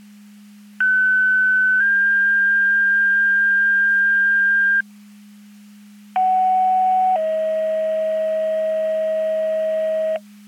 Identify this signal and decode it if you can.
Two-tone page, heard as a one-second tone, then a three-second tone, about 1500 then 1600 Hz, about 750 then 640 Hz. It is heard from 0.8 to 10.2 s.
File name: Portage 154.385 Paging